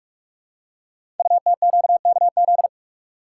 Morse code, heard as a keyed tone, about 44 wpm, 700 Hz. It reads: UTQKZ
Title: Code Groups - 5 Letters - 45wpm